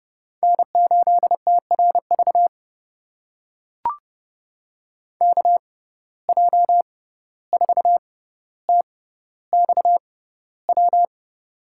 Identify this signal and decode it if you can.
Morse code, heard as a keyed tone, about 30 wpm, spaced as for 15 wpm, 700 Hz. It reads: N8TRV KJ4TXW
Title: USA Call Signs - 6x Character Spacing - 30wpm